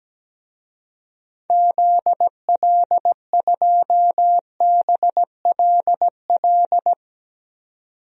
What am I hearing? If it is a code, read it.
Morse code, heard as a keyed tone, about 17 wpm, 700 Hz. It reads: ZL2BLL
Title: DXCC-100 Call Signs - 6 Characters - 17wpm